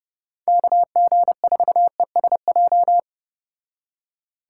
Morse code, heard as a keyed tone, about 34 wpm, 700 Hz. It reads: KG4ESJ